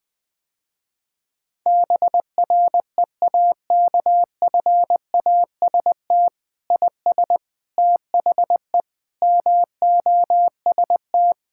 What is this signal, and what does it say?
Morse code, heard as a keyed tone, about 20 wpm, 700 Hz. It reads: BREAKFAST IS THE MOST
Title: Sentences with 9-Word Phrases - Spoken Beginning - 20wpm